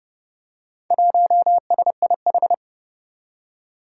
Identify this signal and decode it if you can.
Morse code, heard as a keyed tone, about 30 wpm, 700 Hz. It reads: 1SIH
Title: Four Challenging Letter-Number Combinations 30wpm